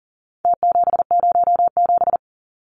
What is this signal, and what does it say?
Morse code, heard as a keyed tone, about 40 wpm, 700 Hz. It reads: T707
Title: Numbers - Contest Style - 4-Digit 40wpm